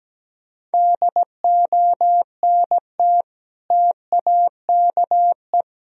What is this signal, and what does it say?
Morse code, heard as a keyed tone, about 17 wpm, 700 Hz. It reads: DONT TAKE
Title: Sentences from Top 1000 Words - Review Part 2 - 17wpm